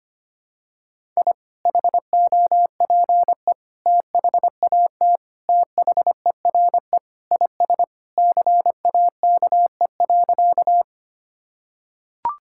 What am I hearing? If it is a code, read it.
Morse code, heard as a keyed tone, about 25 wpm, 700 Hz. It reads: I HOPE THAT THERE IS CAKE.
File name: practice / 25wpm / Sentences with 2-Word Phrases - 25wpm